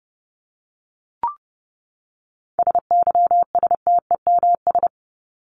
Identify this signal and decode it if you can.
Morse code, heard as a keyed tone, about 30 wpm, 700 Hz. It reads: SYSTEMS
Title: ES:S Suffix - Top 100 Words with Base Words 30wpm